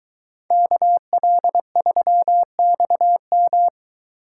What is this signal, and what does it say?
Morse code, heard as a keyed tone, about 22 wpm, 700 Hz. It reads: KL3XM